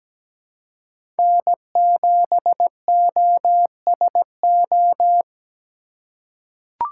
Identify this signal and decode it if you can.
Morse code, heard as a keyed tone, about 17 wpm, 700 Hz. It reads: N7OSO
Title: USA Call Signs - 5 Characters - 2x Character Spacing - 17wpm